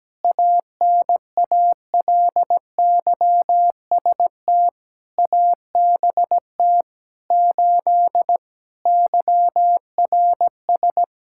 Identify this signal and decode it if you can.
Morse code, heard as a keyed tone, about 17 wpm, 700 Hz. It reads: ANALYST ABT 8 YRS
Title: Practice QSO - Exchange 3 - More Info - 17wpm